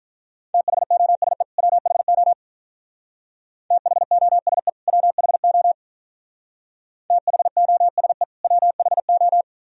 Morse code, heard as a keyed tone, about 44 wpm, 700 Hz. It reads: THOSE WHO THOSE WHO THOSE WHO